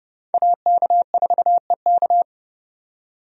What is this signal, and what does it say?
Morse code, heard as a keyed tone, about 30 wpm, 700 Hz. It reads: AK4EK
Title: USA Call Signs - 5 Characters - 30wpm